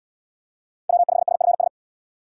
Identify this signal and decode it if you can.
Morse code, heard as a keyed tone, about 77 wpm, 700 Hz. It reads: W5SFS